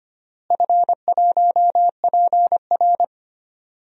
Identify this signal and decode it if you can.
Morse code, heard as a keyed tone, about 25 wpm, 700 Hz. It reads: F1PR
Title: DXCC-100 Call Signs - 4 Characters - Repeated 3x - 25wpm